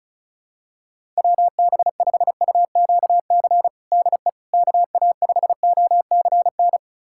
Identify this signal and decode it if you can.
Morse code, heard as a keyed tone, about 36 wpm, 700 Hz. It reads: WB5UQC DE KA5OCN
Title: USA Callsign DE Callsign - 6 Characters - 35wpm